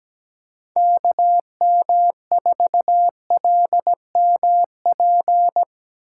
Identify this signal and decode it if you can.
Morse code, heard as a keyed tone, about 17 wpm, 700 Hz. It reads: KM4LMP
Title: USA Call Signs - 6 Characters - 3x Character Spacing - 17wpm